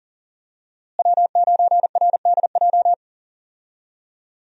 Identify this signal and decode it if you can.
Morse code, heard as a keyed tone, about 40 wpm, 700 Hz. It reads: W9RDJ
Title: USA Call Signs - 5 Characters - Repeated 2x - 40wpm